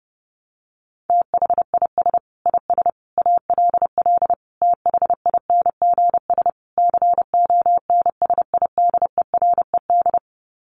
Morse code, heard as a keyed tone, about 28 wpm, 700 Hz. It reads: THIS IS ALL THINGS CONSIDERED
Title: Sets of 5 Words 30wpm